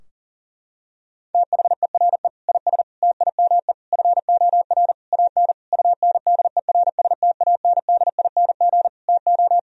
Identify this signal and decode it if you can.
Morse code, heard as a keyed tone, about 40 wpm, 700 Hz. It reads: THERE IS TIME FOR AN UNDERSTANDING TO
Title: Sentences from Top 1000 Words - Review Part 3 - 40wpm